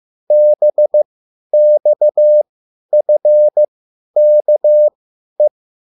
Morse code, heard as a keyed tone, about 15 wpm, spaced as for 10 wpm, 600 Hz. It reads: BXFKE